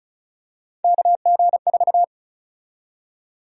Morse code, heard as a keyed tone, about 34 wpm, 700 Hz. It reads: KG4
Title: USA Call Sign Prefixes - 35wpm